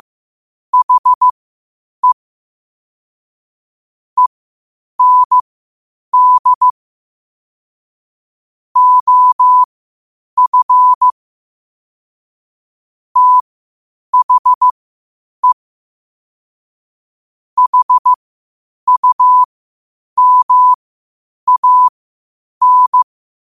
Morse code, heard as a keyed tone, about 15 wpm, spaced as for 8 wpm, 1000 Hz. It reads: HE END OF THE HUMAN
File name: qotdfiles / 8wpm